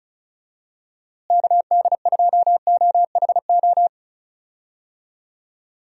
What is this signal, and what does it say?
Morse code, heard as a keyed tone, about 36 wpm, 700 Hz. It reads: KD2OHO
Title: USA Call Signs - 6 Characters - Repeated 2x - 35wpm